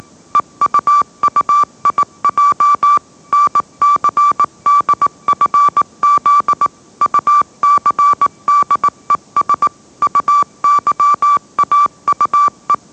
Morse code, heard as a keyed tone, about 18 wpm, 1160 Hz. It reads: EUUIJ NCDFZ UCDES UYAUE
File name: mrx / mrx082 20wpm